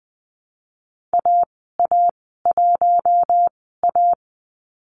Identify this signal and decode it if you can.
Morse code, heard as a keyed tone, about 20 wpm, spaced as for 15 wpm, 700 Hz. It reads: AA1A
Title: 048 Lesson 9g - Number 4 - Call Signs 20wpm